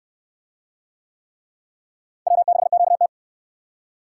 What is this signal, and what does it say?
Morse code, heard as a keyed tone, about 67 wpm, 700 Hz. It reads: W6QT